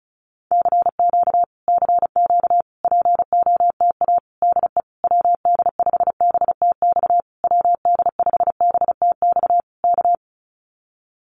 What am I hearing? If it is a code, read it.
Morse code, heard as a keyed tone, about 34 wpm, 700 Hz. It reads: CQ CQ POTA DE WD5BTX WD5BTX K
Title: Common QSO Phrases Encore 35wpm